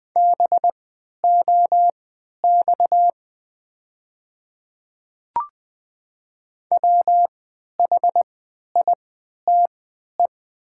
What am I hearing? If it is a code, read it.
Morse code, heard as a keyed tone, about 20 wpm, spaced as for 11 wpm, 700 Hz. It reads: BOX WHITE
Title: 3x Character Spacing - Top 500 Words - 20wpm